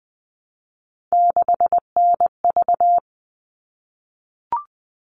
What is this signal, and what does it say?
Morse code, heard as a keyed tone, about 20 wpm, 700 Hz. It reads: 6NV
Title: All Cut Numbers - 3-Digit 20wpm